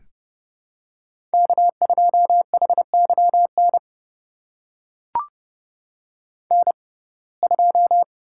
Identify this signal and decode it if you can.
Morse code, heard as a keyed tone, about 29 wpm, spaced as for 21 wpm, 700 Hz. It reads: K2HYN N2
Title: USA Call Signs - 5 Characters - 6x Character Spacing - 30wpm